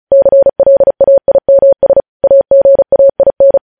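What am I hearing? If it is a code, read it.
Morse code, heard as a keyed tone, about 36 wpm, 550 Hz. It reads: CLAIMS AGAIN
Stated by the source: News headlines sent in Morse at speed 35.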